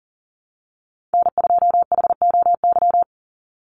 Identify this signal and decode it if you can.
Morse code, heard as a keyed tone, about 40 wpm, 700 Hz. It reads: N2HOY
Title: USA Call Signs 40wpm